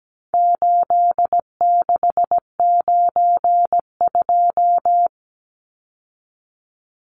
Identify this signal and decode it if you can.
Morse code, heard as a keyed tone, about 17 wpm, 700 Hz. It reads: 8692
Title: Numbers - 4-Digit 17wpm